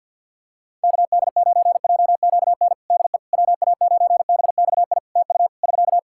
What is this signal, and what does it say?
Morse code, heard as a keyed tone, about 51 wpm, 700 Hz. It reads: KD9JQN DE WA9BXI TU <SK>
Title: Practice QSO - Exchange 4 - Ending - 50wpm